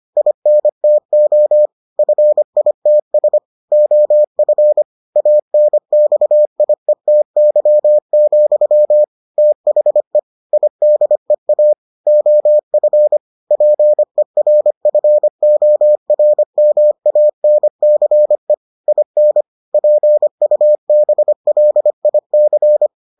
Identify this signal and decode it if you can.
Morse code, heard as a keyed tone, about 25 wpm, 600 Hz. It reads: INTO FITS OF ANXIETY, THE IDEA OF PERFORMANCE IN PUBLIC